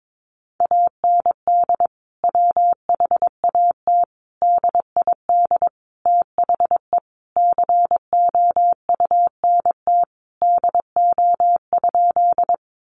Morse code, heard as a keyed tone, about 22 wpm, 700 Hz. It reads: AND WHAT DID THE COUNT DO?
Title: Sentences from Top 1500 Words 22wpm